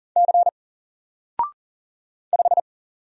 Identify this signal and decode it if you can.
Morse code, heard as a keyed tone, about 40 wpm, 700 Hz. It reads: C 5
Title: Single Letter-Number Alt 40wpm